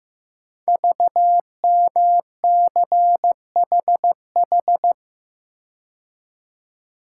Morse code, heard as a keyed tone, about 15 wpm, 700 Hz. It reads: VMCHH
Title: Code Groups - 5 Letter:Numbers - 15wpm